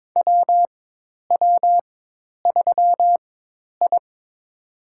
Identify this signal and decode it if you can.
Morse code, heard as a keyed tone, about 22 wpm, spaced as for 10 wpm, 700 Hz. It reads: WW3I
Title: USA Call Signs - 4 Characters - 4x Character Spacing - 22wpm